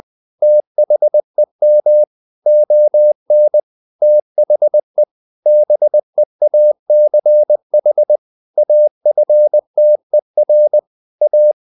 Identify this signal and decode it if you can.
Morse code, heard as a keyed tone, about 20 wpm, 600 Hz. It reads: THEM ON THE BEACH AFTER A
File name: the old man and the sea / the old man and the sea part 1 20wpm